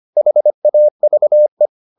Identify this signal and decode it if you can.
Morse code, heard as a keyed tone, about 25 wpm, 600 Hz. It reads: HAVE